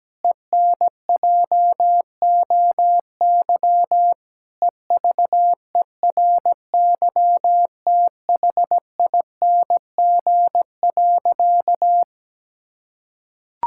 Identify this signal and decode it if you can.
Morse code, heard as a keyed tone, about 17 wpm, 700 Hz. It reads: ENJOY EVERYTHING.
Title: Sentences from Top 1000 Words - Review Part 5 - 17wpm